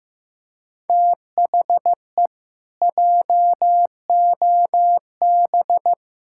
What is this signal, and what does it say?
Morse code, heard as a keyed tone, about 15 wpm, 700 Hz. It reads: THE JOB
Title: Pangram Challenge v2 - 15wpm